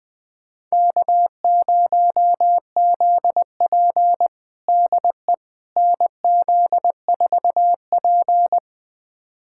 Morse code, heard as a keyed tone, about 21 wpm, 700 Hz. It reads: K0ZP DE NZ4P
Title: USA Callsign DE Callsign - 4 Characters - 20wpm